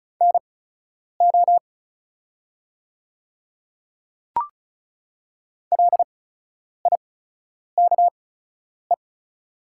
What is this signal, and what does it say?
Morse code, heard as a keyed tone, about 36 wpm, spaced as for 10 wpm, 700 Hz. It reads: NO LIKE
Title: 8x Character Spacing - Top 100 Words - 35wpm